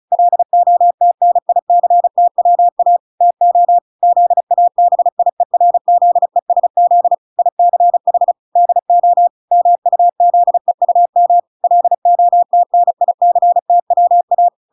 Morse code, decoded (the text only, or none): LOTNICTWA TO ZABIERZESZ ICH DO MUZEUM LOTNICTWA